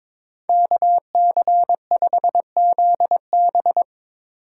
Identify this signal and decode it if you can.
Morse code, heard as a keyed tone, about 21 wpm, 700 Hz. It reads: KC5ZB
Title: USA Call Signs - 8x Character Spacing - 22wpm